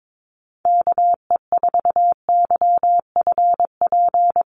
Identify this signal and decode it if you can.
Morse code, heard as a keyed tone, about 22 wpm, 700 Hz. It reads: KE4YFP
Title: USA Call Signs 22wpm